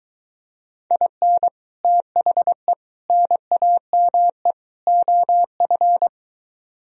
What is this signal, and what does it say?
Morse code, heard as a keyed tone, about 23 wpm, 700 Hz. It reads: IN THE NAME OF